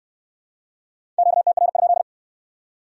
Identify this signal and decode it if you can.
Morse code, heard as a keyed tone, about 60 wpm, 700 Hz. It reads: QTR?